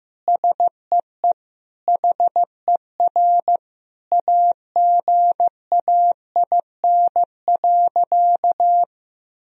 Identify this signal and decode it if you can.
Morse code, heard as a keyed tone, about 15 wpm, 700 Hz. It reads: SEE HER AGAIN.
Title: Sentences from Top 1000 Words - Review Part 3 - 15wpm